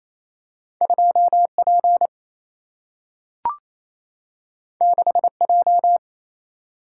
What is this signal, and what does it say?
Morse code, heard as a keyed tone, about 28 wpm, 700 Hz. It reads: 2 P 6 J
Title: Single-Letter Single-Number 28wpm